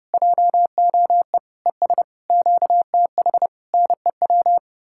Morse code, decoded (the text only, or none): JOE ES QTH NEW